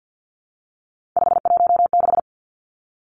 Morse code, heard as a keyed tone, about 50 wpm, 700 Hz. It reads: 516
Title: Three Challenging Letter-Number Combinations 50wpm